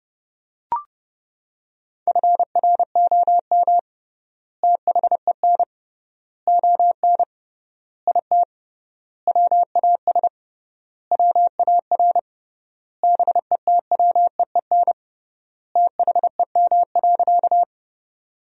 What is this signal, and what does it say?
Morse code, heard as a keyed tone, about 30 wpm, 700 Hz. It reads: FROM THEN ON IT WAS WAR BETWEEN THEM.
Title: 3x Word-Spacing - Sentences from Top 400 Words - 30wpm